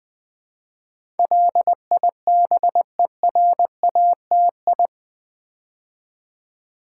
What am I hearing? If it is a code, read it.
Morse code, heard as a keyed tone, about 20 wpm, 700 Hz. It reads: LIBERATI
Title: Build A Word - 4 Letters At A Time - 10-Letter Words - 20wpm